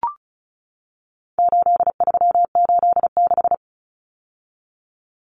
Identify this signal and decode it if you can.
Morse code, heard as a keyed tone, about 34 wpm, 700 Hz. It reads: 8386
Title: Numbers - 4-Digit 35wpm